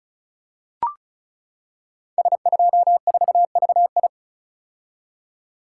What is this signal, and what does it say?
Morse code, heard as a keyed tone, about 36 wpm, 700 Hz. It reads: S24VI
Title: Code Groups - 5 Letter:Numbers - 35wpm